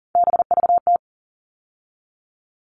Morse code, heard as a keyed tone, about 40 wpm, 700 Hz. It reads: BVT
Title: All Cut Numbers - 3-Digit 40wpm